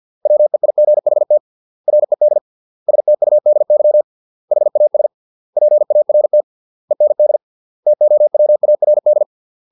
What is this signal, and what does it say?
Morse code, heard as a keyed tone, about 50 wpm, 600 Hz. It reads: WEIGHT RED STUDY HAS PART END TOWARD